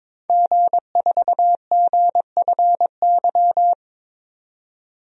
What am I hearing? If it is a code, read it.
Morse code, heard as a keyed tone, about 22 wpm, 700 Hz. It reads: G4GFY
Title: DXCC-100 Call Signs - 5 Characters - 22wpm